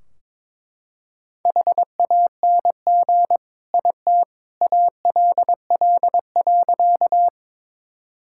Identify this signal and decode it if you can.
Morse code, heard as a keyed tone, about 22 wpm, 700 Hz. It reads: HANG IT ALL.
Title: Sentences from Top 1600 Words 22wpm